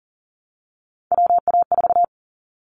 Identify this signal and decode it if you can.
Morse code, heard as a keyed tone, about 40 wpm, 700 Hz. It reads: WA4
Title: Callsign Prefixes for Top 25 DXCC - 40wpm